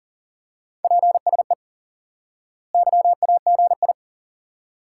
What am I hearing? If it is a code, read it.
Morse code, heard as a keyed tone, about 39 wpm, 700 Hz. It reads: PSE YAGI
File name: IQE Course 3 - Lesson 89 - YAGI - Two Elements - 40wpm